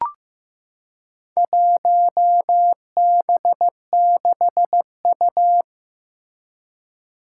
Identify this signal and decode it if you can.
Morse code, heard as a keyed tone, about 15 wpm, 700 Hz. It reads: 1B6U
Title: Four Challenging Letter-Number Combinations 15wpm